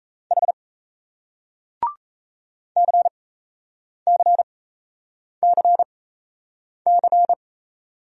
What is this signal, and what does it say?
Morse code, heard as a keyed tone, about 38 wpm, spaced as for 8 wpm, 700 Hz. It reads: H CCCC